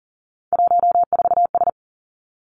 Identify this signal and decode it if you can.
Morse code, heard as a keyed tone, about 39 wpm, 700 Hz. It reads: 14S
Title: Three Challenging Letter-Number Combinations 40wpm